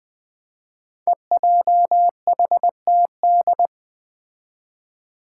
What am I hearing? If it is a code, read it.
Morse code, heard as a keyed tone, about 20 wpm, 700 Hz. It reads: EJHTD
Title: Five Challenging Letter-Number Combinations 20wpm